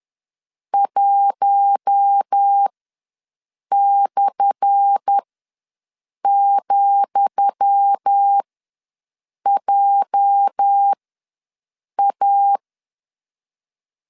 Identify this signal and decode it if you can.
Morse code, heard as a keyed tone, about 10 wpm, spaced as for 6 wpm, 800 Hz. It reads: 1/,JA
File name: LEZIONI / 10 SIGNS / 10D COMA ws 10wpm 4314